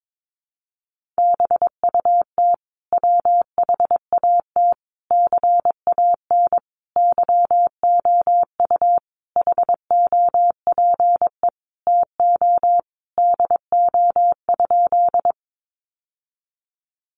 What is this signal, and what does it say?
Morse code, heard as a keyed tone, about 22 wpm, 700 Hz. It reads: BUT WHAT CAN YOU HOPE TO DO?
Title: Sentences from Top 1000 Words 22wpm